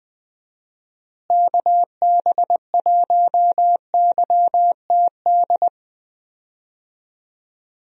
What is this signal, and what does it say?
Morse code, heard as a keyed tone, about 20 wpm, 700 Hz. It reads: KB1YTD
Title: USA Call Signs - 6 Characters - Repeated 2x - 20wpm